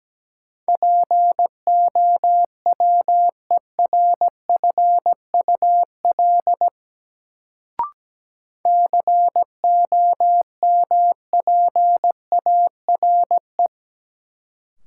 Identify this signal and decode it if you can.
Morse code, heard as a keyed tone, about 17 wpm, 700 Hz. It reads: POWERFUL COMPARE
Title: Sentences from Top 1300 Words - New Word Drill Alt - 17wpm